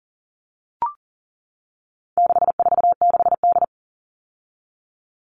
Morse code, heard as a keyed tone, about 40 wpm, 700 Hz. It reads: 646D
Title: All Cut Numbers - 4-Digit 40wpm